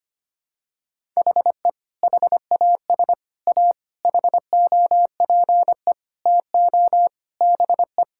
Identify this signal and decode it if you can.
Morse code, heard as a keyed tone, about 25 wpm, 700 Hz. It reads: HE HAS A HOPE TO BE
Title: Sentences from Top 1000 Words - Review Part 6 - 25wpm